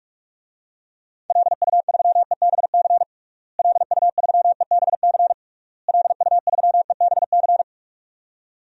Morse code, heard as a keyed tone, about 46 wpm, 700 Hz. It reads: LU3EBC LU3EBC LU3EBC
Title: DXCC-100 Call Signs - 6 Characters - Repeated 3x - 45wpm